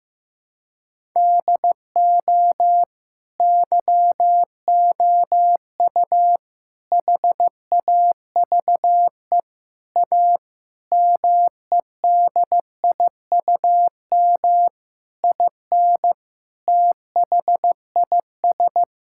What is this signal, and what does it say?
Morse code, heard as a keyed tone, about 15 wpm, 700 Hz. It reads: DO YOU HAVE A MEDIUM IN THIS
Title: Sentences from Top 1000 Words - Review Part 6 - 15wpm